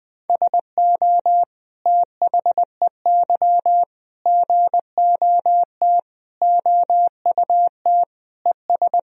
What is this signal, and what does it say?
Morse code, heard as a keyed tone, about 20 wpm, 700 Hz. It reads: SO THEY GOT OUT ES
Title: Sentences from Top 300 Words and Abbreviations 20wpm